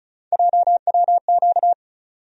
Morse code, heard as a keyed tone, about 36 wpm, 700 Hz. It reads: JWQ
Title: USA Call Sign Postfixes - 3 Letters 35wpm